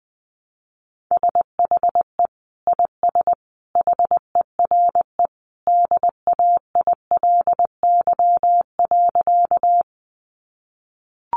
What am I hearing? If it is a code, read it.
Morse code, heard as a keyed tone, about 19 wpm, 700 Hz. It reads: SHE IS HERE DAILY.
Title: Sentences from Top 1100 Words 20wpm